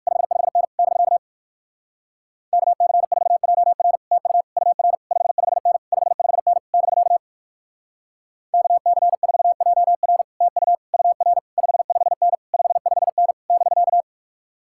Morse code, heard as a keyed tone, about 58 wpm, 700 Hz. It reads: 55N <BK> KC4JR TU UR 55N 55N <BK> KC4JR TU UR 55N 55N <BK>